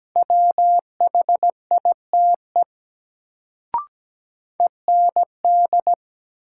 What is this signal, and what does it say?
Morse code, heard as a keyed tone, about 17 wpm, 700 Hz. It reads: WHITE END
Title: Sets of 2 Words - New Word Drill - 17wpm